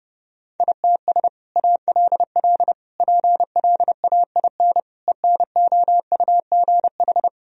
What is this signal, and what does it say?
Morse code, heard as a keyed tone, about 32 wpm, 700 Hz. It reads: ITS ALL PLAIN ENOUGH
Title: Sentences from Top 1000 Words - Review Part 2 - 30wpm